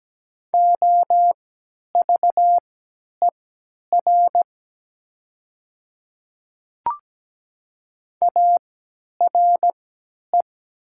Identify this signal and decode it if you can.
Morse code, heard as a keyed tone, about 17 wpm, spaced as for 10 wpm, 700 Hz. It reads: OVER ARE
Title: 3x Character Spacing - Top 100 Words - 17wpm